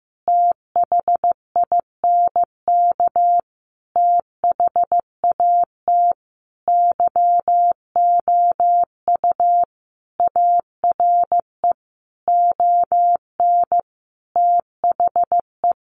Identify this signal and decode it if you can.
Morse code, heard as a keyed tone, about 15 wpm, 700 Hz. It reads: THINK THAT YOU ARE ON THE
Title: Sentences from Top 1400 Words 15wpm